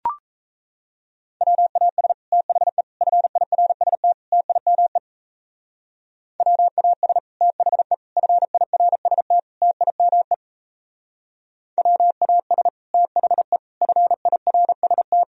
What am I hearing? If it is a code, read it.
Morse code, heard as a keyed tone, about 35 wpm, 700 Hz. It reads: WAS THE FIRST TIME WAS THE FIRST TIME WAS THE FIRST